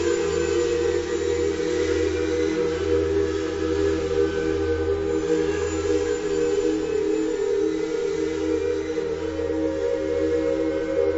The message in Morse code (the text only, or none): none